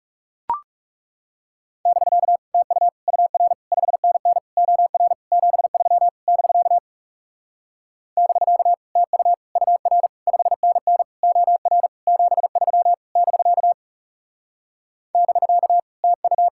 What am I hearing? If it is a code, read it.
Morse code, heard as a keyed tone, about 38 wpm, 700 Hz. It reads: <BK> TU UR 5NN OR 73 <BK> <BK> TU UR 5NN OR 73 <BK> <BK> TU